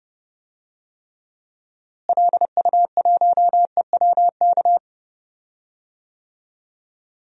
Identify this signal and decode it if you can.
Morse code, heard as a keyed tone, about 30 wpm, 700 Hz. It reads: LU1EWK
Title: DXCC-100 Call Signs - 6 Characters - Repeated 3x - 30wpm